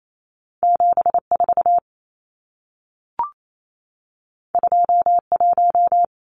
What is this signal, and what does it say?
Morse code, heard as a keyed tone, about 28 wpm, 700 Hz. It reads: 74 21
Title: Numbers - 2-Digit 28wpm